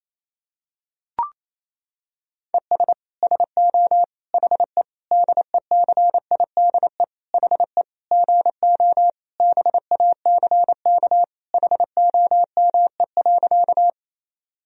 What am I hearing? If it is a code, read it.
Morse code, heard as a keyed tone, about 28 wpm, 700 Hz. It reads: ES SO HE DECIDE HE GO BACK HOME.
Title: Sentences from Top 500 Words and Abbreviations 28wpm